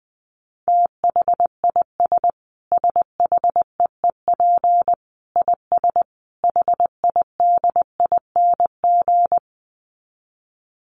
Morse code, heard as a keyed tone, about 20 wpm, 700 Hz. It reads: THIS SHEEP IS HIDING
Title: Dit Challenge 20wpm